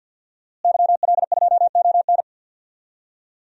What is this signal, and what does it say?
Morse code, heard as a keyed tone, about 51 wpm, 700 Hz. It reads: YL2ON
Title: DXCC-100 Call Signs - 5 Characters - Repeated 2x - 50wpm